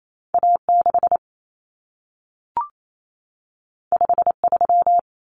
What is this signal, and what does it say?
Morse code, heard as a keyed tone, about 28 wpm, 700 Hz. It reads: A6 53
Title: Common Cut Numbers - 2-Digit 28wpm